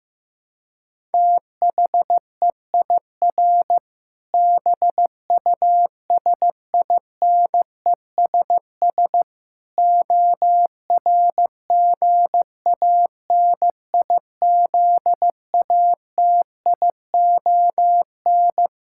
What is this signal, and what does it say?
Morse code, heard as a keyed tone, about 15 wpm, 700 Hz. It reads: THEIR BUSINESS ORGANIZATION
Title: Sentences from Top 1000 Words - Review Part 5 - 15wpm